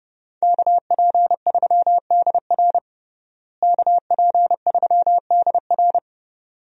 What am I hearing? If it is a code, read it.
Morse code, heard as a keyed tone, about 32 wpm, 700 Hz. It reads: KP3DR KP3DR
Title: USA Call Signs - 5 Characters - Repeated 2x - 30wpm